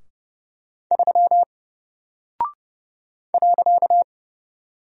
Morse code, heard as a keyed tone, about 30 wpm, 700 Hz. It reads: 3 .
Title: Single Character (Letter:Number:Punctuation:Prosign) Alt 30wpm